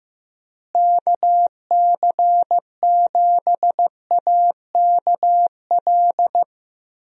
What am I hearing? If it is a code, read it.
Morse code, heard as a keyed tone, about 15 wpm, 700 Hz. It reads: KC7AKL
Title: USA Call Signs - 6 Characters - Repeated 2x - 15wpm